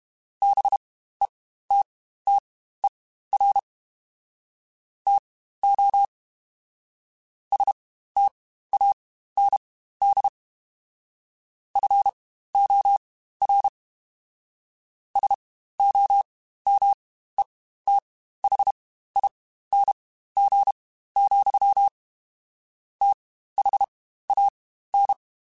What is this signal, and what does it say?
Morse code, heard as a keyed tone, about 32 wpm, spaced as for 15 wpm, 800 Hz. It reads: BETTER TO STAND FOR SOMETHING, THAN
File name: Quotes Spaced / Quotes Spaced-mp3s / StandFor sp @30